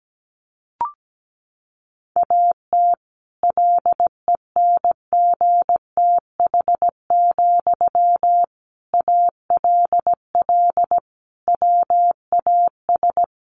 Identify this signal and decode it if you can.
Morse code, heard as a keyed tone, about 17 wpm, 700 Hz. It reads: AT LENGTH, ALL WAS
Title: Sentences from Top 1000 Words 17wpm